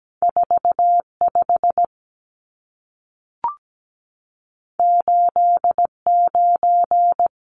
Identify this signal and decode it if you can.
Morse code, heard as a keyed tone, about 17 wpm, 700 Hz. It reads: 45 89
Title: Numbers - 2-Digit 17wpm